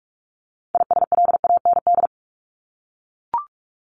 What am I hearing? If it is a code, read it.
Morse code, heard as a keyed tone, about 44 wpm, 700 Hz. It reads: ISLAND
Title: Top 500 Words Repeated 45wpm